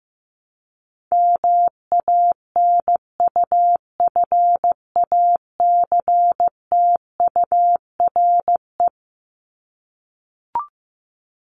Morse code, heard as a keyed tone, about 15 wpm, 700 Hz. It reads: MANUFACTURE
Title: ING Suffix - Top 100 Words with Base Words 15wpm